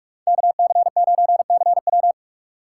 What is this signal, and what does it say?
Morse code, heard as a keyed tone, about 46 wpm, 700 Hz. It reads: KC9CW
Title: USA Call Signs - 6x Character Spacing - 45wpm